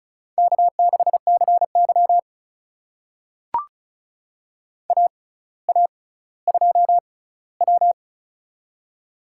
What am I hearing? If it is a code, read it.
Morse code, heard as a keyed tone, about 36 wpm, spaced as for 18 wpm, 700 Hz. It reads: K6CY AA2W
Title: USA Call Signs - 4 Characters - 6x Character Spacing - 35wpm